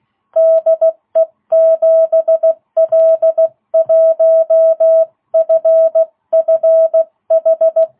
Morse code, heard as a keyed tone, about 16 wpm, 660 Hz. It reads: DE 7L1FFH